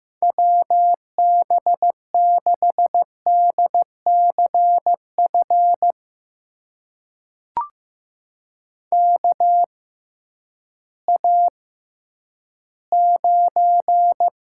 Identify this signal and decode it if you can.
Morse code, heard as a keyed tone, about 15 wpm, spaced as for 10 wpm, 700 Hz. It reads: WB6DCF KA9
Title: USA Call Signs - 6 Characters - 6x Character Spacing - 15wpm